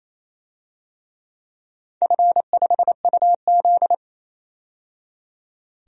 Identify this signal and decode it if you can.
Morse code, heard as a keyed tone, about 28 wpm, 700 Hz. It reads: F5UZ
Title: DXCC-100 Call Signs - 4 Characters - 28wpm